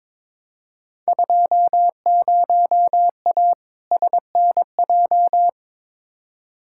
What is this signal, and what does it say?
Morse code, heard as a keyed tone, about 22 wpm, 700 Hz. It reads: 20A SNJ
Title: Field Day Exchanges Encore - 22wpm